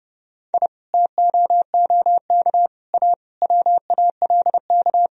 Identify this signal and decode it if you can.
Morse code, heard as a keyed tone, about 31 wpm, 700 Hz. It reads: I TOOK A WALK
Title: Sentences from Top 1000 Words - Review Part 4 - 30wpm